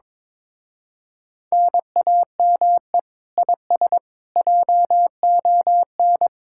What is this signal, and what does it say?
Morse code, heard as a keyed tone, about 22 wpm, 700 Hz. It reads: NAME IS JON
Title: Speed-Racing Common QSO Phrases - 22wpm